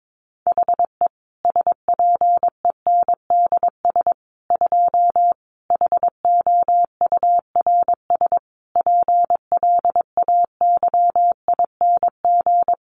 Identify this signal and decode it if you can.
Morse code, heard as a keyed tone, about 22 wpm, 700 Hz. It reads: HE SPENDS 2 HOURS PLAYING
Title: Additional Easy Phrases 22wpm